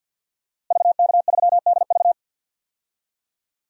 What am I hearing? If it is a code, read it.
Morse code, heard as a keyed tone, about 51 wpm, 700 Hz. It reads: VK3DV
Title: DXCC-100 Call Signs - 5 Characters - Repeated 2x - 50wpm